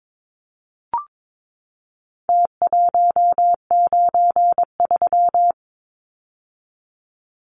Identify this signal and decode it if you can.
Morse code, heard as a keyed tone, about 22 wpm, 700 Hz. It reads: T193
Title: Numbers - Contest Style - 4-Digit 22wpm